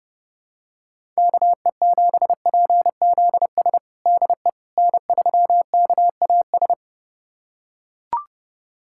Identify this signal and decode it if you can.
Morse code, heard as a keyed tone, about 30 wpm, 700 Hz. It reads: KE7PZS DE N3KAS
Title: USA Callsign DE Callsign 30wpm